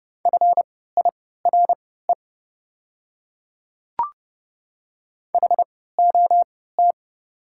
Morse code, heard as a keyed tone, about 30 wpm, spaced as for 17 wpm, 700 Hz. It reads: FIRE HOT
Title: 3x Character Spacing - Top 500 Words - 30wpm